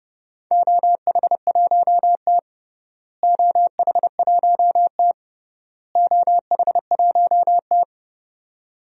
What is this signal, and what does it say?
Morse code, heard as a keyed tone, about 30 wpm, 700 Hz. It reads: OH1T OH1T OH1T
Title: DXCC-100 Call Signs - 4 Characters - Repeated 3x - 30wpm